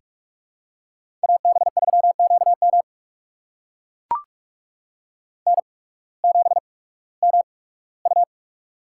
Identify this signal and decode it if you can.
Morse code, heard as a keyed tone, about 44 wpm, spaced as for 21 wpm, 700 Hz. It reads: AB3QM N7MU
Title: USA Call Signs - 8x Character Spacing - 45wpm